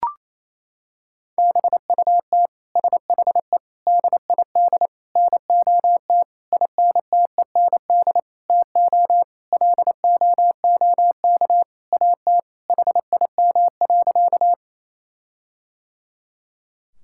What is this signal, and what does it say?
Morse code, heard as a keyed tone, about 28 wpm, 700 Hz. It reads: BUT SHE DID NOT INTEND TO LOOK AT HIM.
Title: Sentences from Top 1600 Words 28wpm